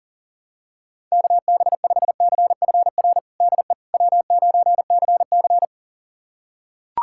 Morse code, heard as a keyed tone, about 40 wpm, 700 Hz. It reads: KB5CFR DE W9CC
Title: USA Callsign DE Callsign 40wpm